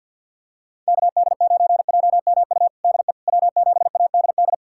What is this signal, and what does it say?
Morse code, heard as a keyed tone, about 45 wpm, 700 Hz. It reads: KD9JMU DE W7ADD
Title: USA Callsign DE Callsign 50wpm